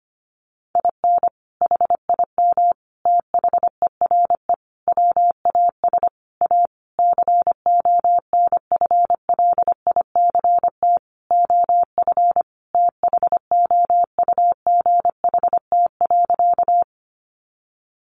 Morse code, heard as a keyed tone, about 25 wpm, 700 Hz. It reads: IN HIM THERE WAS A CONFLICT OF THOUGHT.
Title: Sentences from Top 1300 Words 25wpm